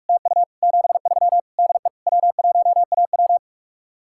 Morse code, heard as a keyed tone, about 49 wpm, 700 Hz. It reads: TU 73 DE W1AW